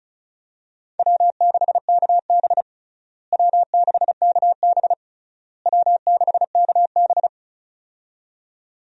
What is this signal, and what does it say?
Morse code, heard as a keyed tone, about 36 wpm, 700 Hz. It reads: W6KB W6KB W6KB
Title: USA Call Signs - 4 Characters - Repeated 3x - 35wpm